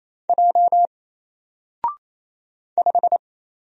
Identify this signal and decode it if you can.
Morse code, heard as a keyed tone, about 27 wpm, 700 Hz. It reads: J 5
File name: Single-Letter-Number-28wpm